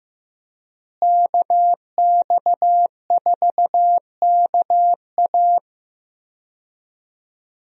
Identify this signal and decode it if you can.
Morse code, heard as a keyed tone, about 15 wpm, 700 Hz. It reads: KX4KA
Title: USA Call Signs - 5 Characters - Repeated 2x - 15wpm